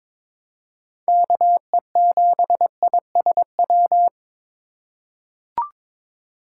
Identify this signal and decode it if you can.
Morse code, heard as a keyed tone, about 22 wpm, 700 Hz. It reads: KE7ISW
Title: USA Call Signs - 6 Characters - 2x Character Spacing - 22wpm